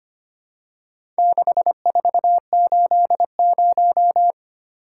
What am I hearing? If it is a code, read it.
Morse code, heard as a keyed tone, about 25 wpm, 700 Hz. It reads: 6480
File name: practice / 25wpm / Four Letter-Number Combinations - 25wpm